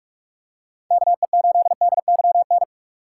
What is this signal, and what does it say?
Morse code, heard as a keyed tone, about 46 wpm, 700 Hz. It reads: KE8DYN
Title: USA Call Signs - 6 Characters - 4x Character Spacing - 45wpm